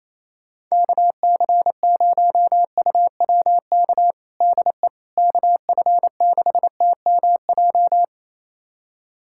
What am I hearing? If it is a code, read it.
Morse code, heard as a keyed tone, about 28 wpm, 700 Hz. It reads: KC0UWK DE KF6TMJ
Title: USA Callsign DE Callsign - 6 Characters - 28wpm